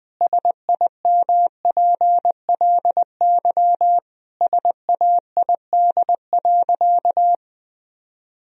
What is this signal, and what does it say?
Morse code, heard as a keyed tone, about 20 wpm, 700 Hz. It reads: SIMPLY SAID.
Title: Sentences from Top 1000 Words - Review Part 4 - 20wpm